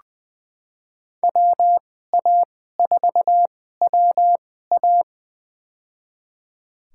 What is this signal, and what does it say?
Morse code, heard as a keyed tone, about 21 wpm, spaced as for 15 wpm, 700 Hz. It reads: WA4WA
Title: USA Call Signs - 5 Characters - 2x Character Spacing - 20wpm